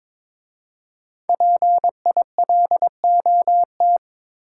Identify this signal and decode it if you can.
Morse code, heard as a keyed tone, about 22 wpm, 700 Hz. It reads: PILOT
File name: 5 Letter Words - First 3 Letters Given - 22wpm